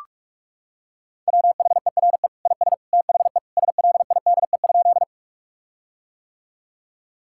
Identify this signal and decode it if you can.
Morse code, heard as a keyed tone, about 46 wpm, 700 Hz. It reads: WHERE IS THE SLIDE?
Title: Sentences from Top 1600 Words 45wpm